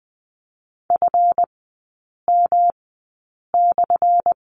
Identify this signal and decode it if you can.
Morse code, heard as a keyed tone, about 20 wpm, spaced as for 8 wpm, 700 Hz. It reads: FM/
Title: 178 Lesson 29e - Slash - Three Characters 20wpm